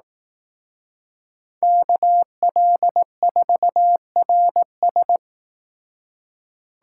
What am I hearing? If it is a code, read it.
Morse code, heard as a keyed tone, about 18 wpm, 700 Hz. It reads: KL4RS
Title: Speed Racing - USA 2x2 Call Signs 15wpm